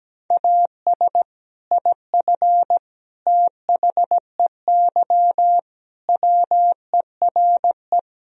Morse code, heard as a keyed tone, about 17 wpm, 700 Hz. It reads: AS IF THEY WERE
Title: Speed-Racing - Sets of 4 Words 17wpm